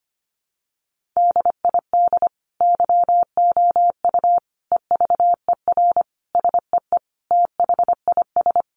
Morse code, heard as a keyed tone, about 25 wpm, 700 Hz. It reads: DID YOU EVER SEE THIS
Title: Sentences from Top 1500 Words 25wpm